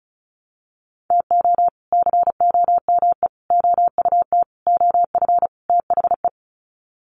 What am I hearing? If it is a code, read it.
Morse code, heard as a keyed tone, about 34 wpm, 700 Hz. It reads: TO COME OUT OF THE
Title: Sets of 5 Words Encore2 35wpm